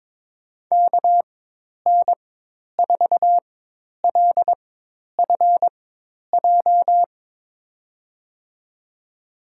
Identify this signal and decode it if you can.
Morse code, heard as a keyed tone, about 22 wpm, spaced as for 10 wpm, 700 Hz. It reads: KN4LFJ
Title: USA Call Signs - 6 Characters - 4x Character Spacing - 22wpm